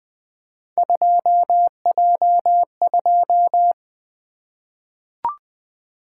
Morse code, heard as a keyed tone, about 20 wpm, 700 Hz. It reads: 2J2
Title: Three Letter-Number Combinations - 20wpm